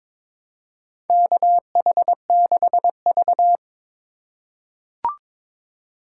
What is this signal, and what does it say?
Morse code, heard as a keyed tone, about 22 wpm, 700 Hz. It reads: KH6V
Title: USA Call Signs - 4 Characters - 8x Character Spacing - 22wpm